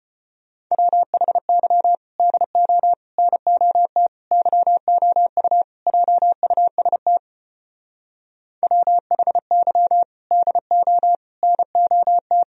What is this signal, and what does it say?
Morse code, heard as a keyed tone, about 31 wpm, 700 Hz. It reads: WHY DO NOT YOU JUST WHY DO NOT
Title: Speed-Racing - Sets of 5 Words 25wpm